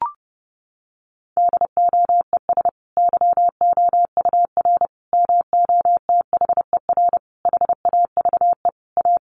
Morse code, heard as a keyed tone, about 30 wpm, 700 Hz. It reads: DOES YOUR MOTHER HAVE A
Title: Sentences from Top 1400 Words 30wpm